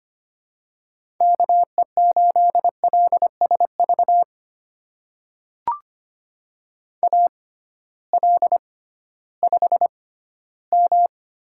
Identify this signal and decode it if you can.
Morse code, heard as a keyed tone, about 25 wpm, spaced as for 14 wpm, 700 Hz. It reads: KE8LSV AL5M
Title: USA Call Signs - 6x Character Spacing - 25wpm